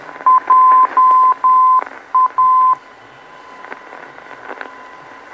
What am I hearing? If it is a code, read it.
Morse code, heard as a keyed tone, about 10 wpm, 1030 Hz, with radio noise. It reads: JA
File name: beacon program / tracks / 01 JA Beacon